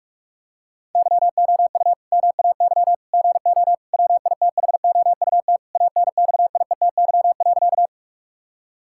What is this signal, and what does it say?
Morse code, heard as a keyed tone, about 46 wpm, 700 Hz. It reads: YOU MAY GO WITHOUT ANXIETY.
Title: Sentences from Top 1600 Words 45wpm